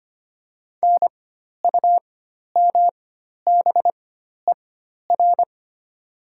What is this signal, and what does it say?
Morse code, heard as a keyed tone, about 25 wpm, spaced as for 12 wpm, 700 Hz. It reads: NUMBER
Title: 4x Character Spacing - Top 200 Words - 25wpm